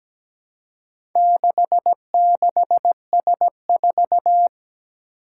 Morse code, heard as a keyed tone, about 17 wpm, 700 Hz. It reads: 66S4
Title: Four Challenging Letter-Number Combinations 17wpm